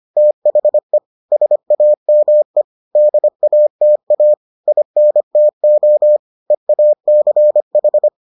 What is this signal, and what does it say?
Morse code, heard as a keyed tone, about 25 wpm, 600 Hz. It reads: THE SAME DATA INTO EACH